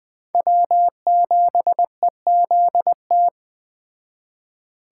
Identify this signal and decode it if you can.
Morse code, heard as a keyed tone, about 19 wpm, 700 Hz. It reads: W7EZT